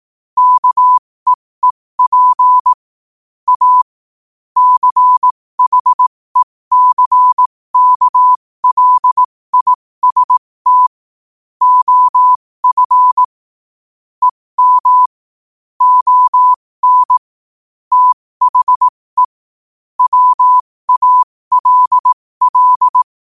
Morse code, heard as a keyed tone, about 18 wpm, spaced as for 15 wpm, 1000 Hz. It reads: KEEP A CHECKLIST OF EM ON THE WALL
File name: qotdfiles / 15wpm